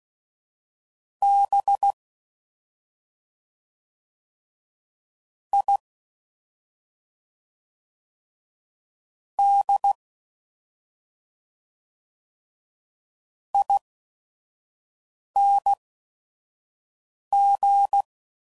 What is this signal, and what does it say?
Morse code, heard as a keyed tone, about 16 wpm, spaced as for 5 wpm, 800 Hz. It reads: B I D ING
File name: Suffixes / Special ING's / SlowToFaster ING's